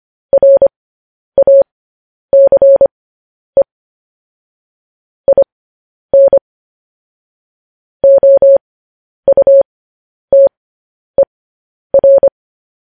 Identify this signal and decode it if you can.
Morse code, heard as a keyed tone, about 25 wpm, spaced as for 10 wpm, 550 Hz. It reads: RACE IN OUTER